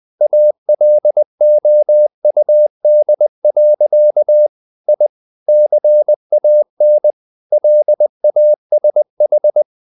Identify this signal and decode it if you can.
Morse code, heard as a keyed tone, about 20 wpm, 600 Hz. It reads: ALOUD. I CAN LASH
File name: the old man and the sea part 2 20wpm